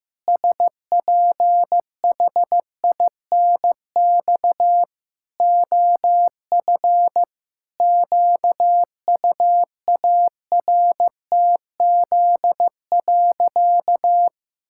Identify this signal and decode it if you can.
Morse code, heard as a keyed tone, about 14 wpm, 700 Hz. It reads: SPHINX OF QUARTZ.
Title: Pangram Challenge v2 - 15wpm